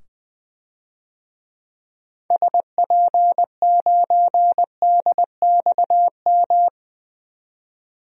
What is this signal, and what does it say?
Morse code, heard as a keyed tone, about 20 wpm, 700 Hz. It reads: SP9DXM
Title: DXCC-100 Call Signs - 6 Characters - 20wpm